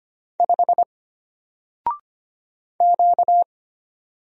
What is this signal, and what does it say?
Morse code, heard as a keyed tone, about 25 wpm, 700 Hz. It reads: 5 Q
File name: Single-Letter-Number-25wpm